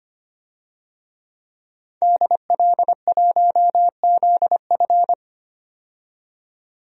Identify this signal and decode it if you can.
Morse code, heard as a keyed tone, about 25 wpm, 700 Hz. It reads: DL1ZF
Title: DXCC-100 Call Signs - 5 Characters - 25wpm